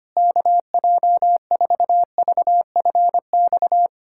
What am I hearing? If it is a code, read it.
Morse code, heard as a keyed tone, about 25 wpm, 700 Hz. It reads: KJ4VFX
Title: USA Callsign DE Callsign - 6 Characters - 25wpm